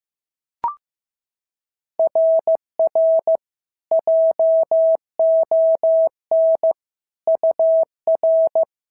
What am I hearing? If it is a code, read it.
Morse code, heard as a keyed tone, about 15 wpm, 690 Hz. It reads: RR JON UR
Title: SOTA QSO Practice 15wpm